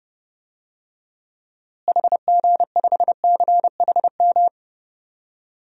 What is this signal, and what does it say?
Morse code, heard as a keyed tone, about 28 wpm, 700 Hz. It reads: HG5CHM
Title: DXCC-100 Call Signs - 6 Characters - Repeated 3x - 30wpm